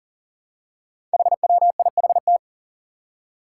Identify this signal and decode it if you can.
Morse code, heard as a keyed tone, about 38 wpm, 700 Hz. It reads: HWIHT
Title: Five Challenging Letter-Number Combinations 40wpm